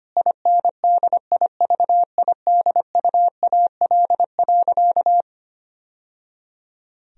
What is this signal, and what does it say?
Morse code, heard as a keyed tone, about 25 wpm, 700 Hz. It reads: INDIVIDUAL.
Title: Sentences from Top 1000 Words - Review Part 4 - 25wpm